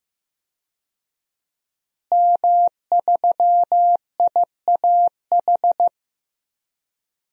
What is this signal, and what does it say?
Morse code, heard as a keyed tone, about 15 wpm, 700 Hz. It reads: M3IAH
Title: DXCC-100 Call Signs - 5 Characters - Repeated 2x - 15wpm